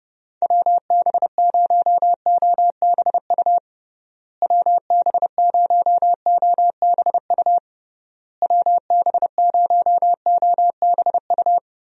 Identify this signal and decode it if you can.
Morse code, heard as a keyed tone, about 28 wpm, 700 Hz. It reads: WB0OBU WB0OBU WB0OBU
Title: USA Call Signs - 6 Characters - Repeated 3x - 30wpm